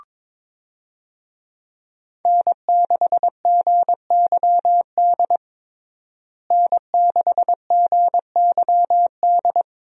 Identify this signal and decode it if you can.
Morse code, heard as a keyed tone, about 22 wpm, 700 Hz. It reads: N6GYD N6GYD
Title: USA Call Signs - 5 Characters - Repeated 2x - 22wpm